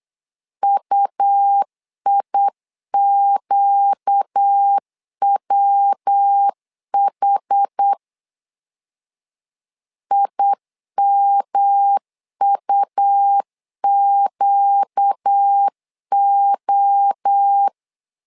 Morse code, heard as a keyed tone, about 8 wpm, 800 Hz. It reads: UIQWH IMUQO
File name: LEZIONI / 4 Q1S3 8wpm 3037